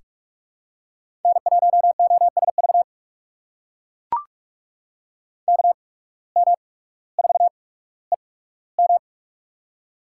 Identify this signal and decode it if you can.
Morse code, heard as a keyed tone, about 42 wpm, spaced as for 19 wpm, 700 Hz. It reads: N1OSV KM4EM
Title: USA Call Signs - 5 Characters - 8x Character Spacing - 45wpm